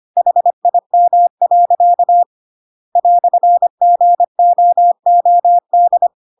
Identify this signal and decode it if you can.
Morse code, heard as a keyed tone, about 25 wpm, 700 Hz. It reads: HIM. "GOOD
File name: Pinocchio / Pinocchio ch2 25 25 ik1hge